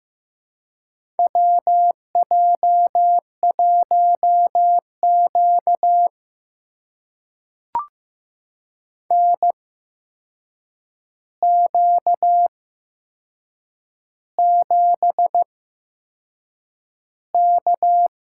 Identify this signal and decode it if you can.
Morse code, heard as a keyed tone, about 15 wpm, spaced as for 6 wpm, 700 Hz. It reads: WJ1Q NQ7K
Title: USA Call Signs - 4 Characters - 8x Character Spacing - 15wpm